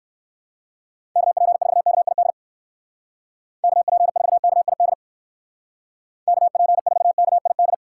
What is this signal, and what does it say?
Morse code, heard as a keyed tone, about 60 wpm, 700 Hz. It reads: KP4KID KP4KID KP4KID